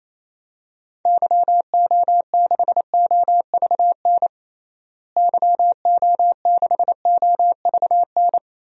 Morse code, heard as a keyed tone, about 28 wpm, 700 Hz. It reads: YO6OVN YO6OVN
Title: DXCC-100 Call Signs - 6 Characters - Repeated 2x - 28wpm